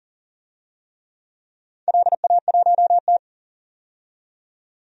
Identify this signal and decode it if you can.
Morse code, heard as a keyed tone, about 40 wpm, 700 Hz. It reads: LA1T
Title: DXCC-100 Call Signs - 4 Characters - 40wpm